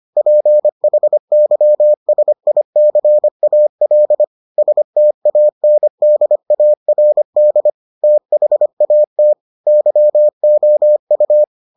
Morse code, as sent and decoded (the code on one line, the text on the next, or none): .--. .... -.-- ... .. -.-. .- .-.. / ... - .- -. -.. .- .-. -.. / - .... .- - / -.-- --- ..-
PHYSICAL STANDARD THAT YOU